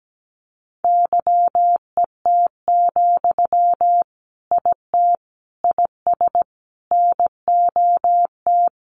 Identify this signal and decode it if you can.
Morse code, heard as a keyed tone, about 17 wpm, 700 Hz. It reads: YET, IT IS NOT
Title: Sentences from Top 1400 Words 17wpm